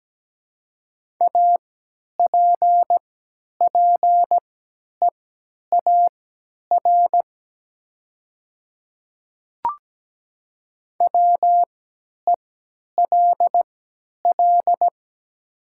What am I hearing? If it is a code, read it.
Morse code, heard as a keyed tone, about 17 wpm, spaced as for 10 wpm, 700 Hz. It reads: APPEAR WELL
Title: 3x Character Spacing - Top 500 Words - 17wpm